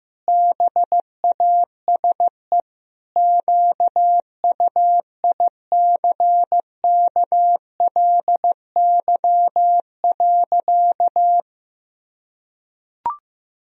Morse code, heard as a keyed tone, about 15 wpm, 700 Hz. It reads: BASE QUICKLY.
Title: Sentences from Top 1000 Words - Review Part 1 - 15wpm